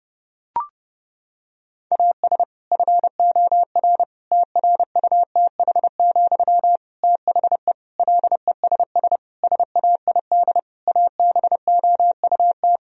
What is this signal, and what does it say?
Morse code, heard as a keyed tone, about 30 wpm, 700 Hz. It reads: AS FOR TRUTH, THE LESS SAID ABOUT
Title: Sentences from Top 1000 Words - Review Part 2 - 30wpm